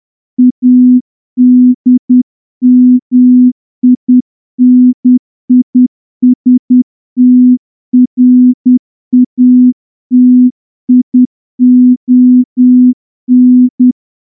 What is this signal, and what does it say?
Morse code, heard as a keyed tone, about 10 wpm, 250 Hz. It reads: ADMINISTRATION
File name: morse-training-10WPM